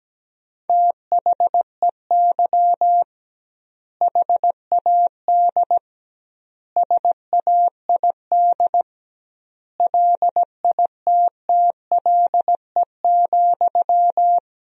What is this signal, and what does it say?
Morse code, heard as a keyed tone, about 17 wpm, 700 Hz. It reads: THEY HAD SAID LITTLE,
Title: 2x Word-Spacing - Sentences from Top 400 Words - 17wpm